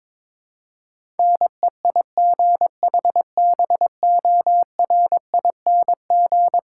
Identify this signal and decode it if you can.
Morse code, heard as a keyed tone, about 22 wpm, 700 Hz. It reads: NEIGHBORING
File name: practice / 22wpm / Eleven Letter Words - 22wpm